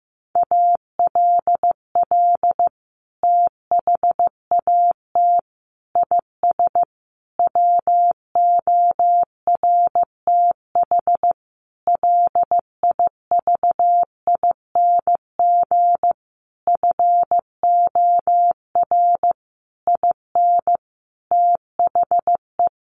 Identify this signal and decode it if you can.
Morse code, heard as a keyed tone, about 15 wpm, 700 Hz. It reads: ALL THAT IS WORTH LIVING FOR IN THE
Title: Sentences from Top 1100 Words 15wpm